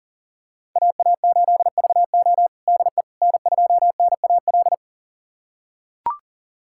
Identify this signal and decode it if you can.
Morse code, heard as a keyed tone, about 40 wpm, 700 Hz. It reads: AA8VO DE N2NAL
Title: USA Callsign DE Callsign - 5 Characters - 40wpm